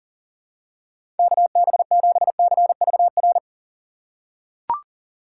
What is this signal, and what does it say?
Morse code, heard as a keyed tone, about 40 wpm, 700 Hz. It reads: KB7CVR
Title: USA Call Signs - 6 Characters - 6x Character Spacing - 40wpm